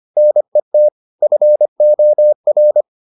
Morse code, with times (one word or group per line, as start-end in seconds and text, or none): 0.2-0.9 NET
1.2-2.8 FOR